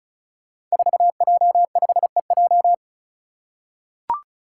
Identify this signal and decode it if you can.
Morse code, heard as a keyed tone, about 34 wpm, 700 Hz. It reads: 4J5EJ
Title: Five Challenging Letter-Number Combinations 35wpm